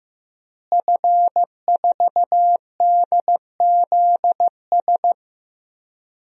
Morse code, heard as a keyed tone, about 15 wpm, 700 Hz. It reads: F4DZS
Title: DXCC-100 Call Signs - 5 Characters - Repeated 3x - 15wpm